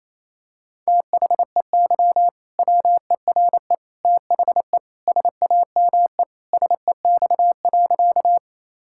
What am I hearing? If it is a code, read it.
Morse code, heard as a keyed tone, about 27 wpm, 700 Hz. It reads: THEY WERE THE SAME SEX.
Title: Sentences from Top 1600 Words 28wpm